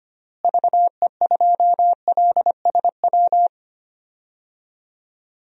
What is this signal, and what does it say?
Morse code, heard as a keyed tone, about 25 wpm, 700 Hz. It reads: VE2LSW
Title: DXCC-100 Call Signs - 6 Characters - Repeated 2x - 25wpm